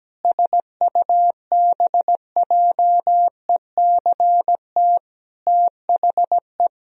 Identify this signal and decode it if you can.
Morse code, heard as a keyed tone, about 17 wpm, 700 Hz. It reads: SUBJECT THE
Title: Sentences from Top 1000 Words - Review Part 2 - 17wpm